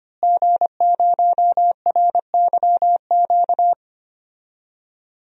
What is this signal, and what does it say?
Morse code, heard as a keyed tone, about 25 wpm, 700 Hz. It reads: G0RYQ
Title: DXCC-100 Call Signs - 5 Characters - Repeated 3x - 25wpm